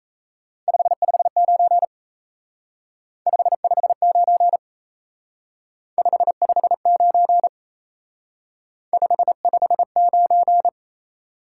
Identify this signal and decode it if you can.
Morse code, heard as a keyed tone, about 34 wpm, 700 Hz. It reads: 559 559 559 559
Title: Speed-Racing Top 90 QSO Elements 28wpm